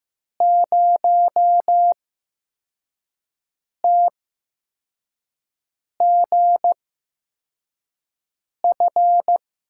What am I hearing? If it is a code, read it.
Morse code, heard as a keyed tone, about 15 wpm, spaced as for 4 wpm, 700 Hz. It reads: 0TGF
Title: USA Call Signs - 5 Characters - 8x Character Spacing - 15wpm